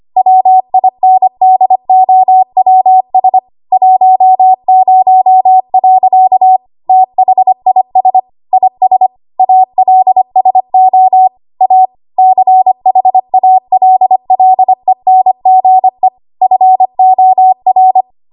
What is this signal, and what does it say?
Morse code, heard as a keyed tone, about 25 wpm, 750 Hz. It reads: WINDOWS 10. THIS IS ALSO A CHALLENGE FOR
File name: Archive / 25 WPM / 25WPM